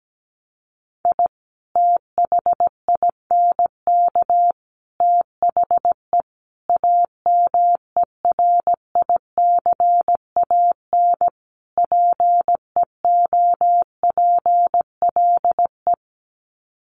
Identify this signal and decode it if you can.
Morse code, heard as a keyed tone, about 17 wpm, 700 Hz. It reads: I THINK THE AMERICAN PEOPLE
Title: Sets of 5 Words Encore 17wpm